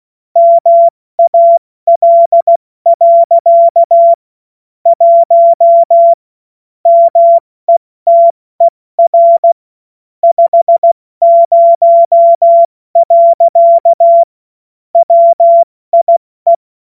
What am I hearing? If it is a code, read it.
Morse code, heard as a keyed tone, about 16 wpm, 680 Hz. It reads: MAL. 1 METER 50. WIE
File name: Morsekurs / Witze / Witz 14. September 2021, 70 F80